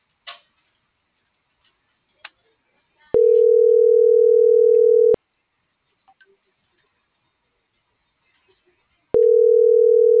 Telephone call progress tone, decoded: ringback tone

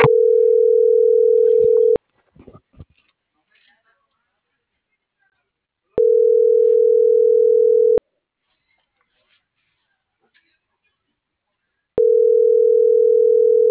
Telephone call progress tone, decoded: ringback tone

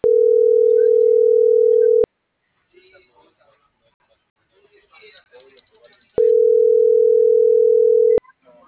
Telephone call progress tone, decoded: ringback tone

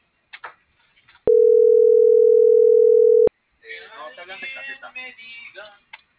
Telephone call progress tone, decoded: ringback tone

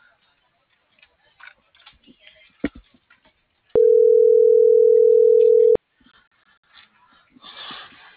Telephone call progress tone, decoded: ringback tone

